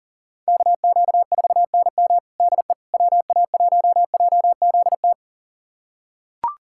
Morse code, heard as a keyed tone, about 40 wpm, 700 Hz. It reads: KQ4NM DE WA1JZT
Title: USA Callsign DE Callsign 40wpm